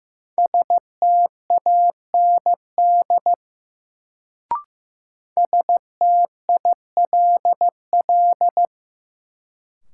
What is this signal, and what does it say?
Morse code, heard as a keyed tone, about 15 wpm, 700 Hz. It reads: STAND STILL
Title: Sentences from Top 300 Words - New Word Drill - 15wpm